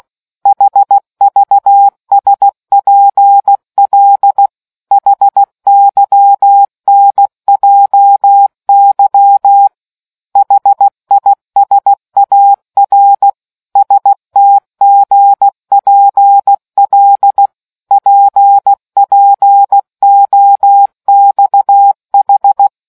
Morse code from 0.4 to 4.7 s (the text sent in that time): HVSPL